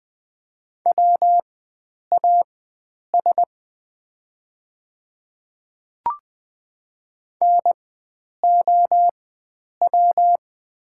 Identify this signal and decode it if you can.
Morse code, heard as a keyed tone, about 20 wpm, spaced as for 9 wpm, 700 Hz. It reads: WAS NOW
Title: 4x Character Spacing - Top 100 Words - 20wpm